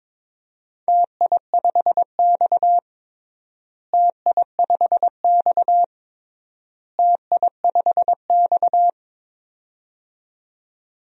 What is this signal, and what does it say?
Morse code, heard as a keyed tone, about 22 wpm, 700 Hz. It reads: TI5X TI5X TI5X
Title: DXCC-100 Call Signs - 4 Characters - Repeated 3x - 22wpm